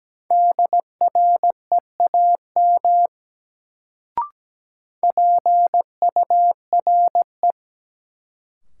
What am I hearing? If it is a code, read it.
Morse code, heard as a keyed tone, about 17 wpm, 700 Hz. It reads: DREAM PURE
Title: Sentences from Top 1500 Words - New Word Drill - 17wpm